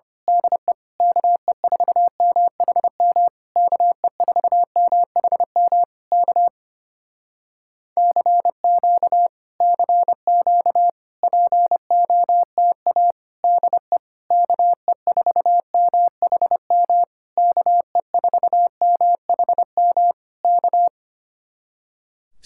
Morse code, heard as a keyed tone, about 24 wpm, 700 Hz. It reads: DE KE4MHM KE4MHM K CQ CQ POTA DE KE4MHM KE4MHM K
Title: Speed-Racing Common QSO Phrases - 25wpm